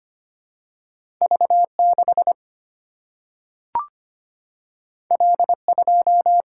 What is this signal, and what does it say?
Morse code, heard as a keyed tone, about 25 wpm, 700 Hz. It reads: V 6 L 2
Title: Single-Letter Single-Number 25wpm